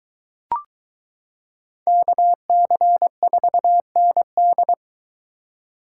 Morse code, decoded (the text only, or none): KC4ND